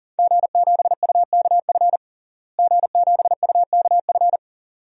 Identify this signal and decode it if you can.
Morse code, heard as a keyed tone, about 40 wpm, 700 Hz. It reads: G7UKF G7UKF
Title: DXCC-100 Call Signs - 5 Characters - Repeated 2x - 40wpm